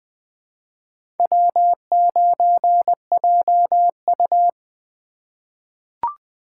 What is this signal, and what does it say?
Morse code, heard as a keyed tone, about 20 wpm, 700 Hz. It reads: W9JU
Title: USA Call Signs - 4 Characters - 2x Character Spacing - 20wpm